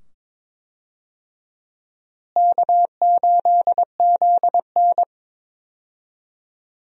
Morse code, heard as a keyed tone, about 22 wpm, 700 Hz. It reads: K8ZN
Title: USA Call Signs - 4 Characters - Repeated 2x - 22wpm